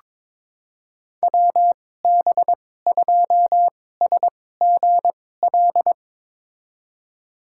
Morse code, heard as a keyed tone, about 22 wpm, spaced as for 16 wpm, 700 Hz. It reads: WB2SGL
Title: USA Call Signs - 6 Characters - 2x Character Spacing - 22wpm